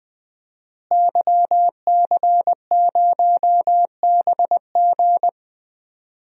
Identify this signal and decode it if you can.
Morse code, heard as a keyed tone, about 20 wpm, 700 Hz. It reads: YC0BG
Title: DXCC-100 Call Signs - 5 Characters - Repeated 2x - 20wpm